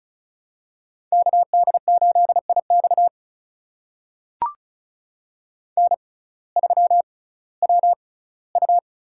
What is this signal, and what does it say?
Morse code, heard as a keyed tone, about 34 wpm, spaced as for 19 wpm, 700 Hz. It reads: KD8IX N3WU
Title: USA Call Signs - 5 Characters - 6x Character Spacing - 35wpm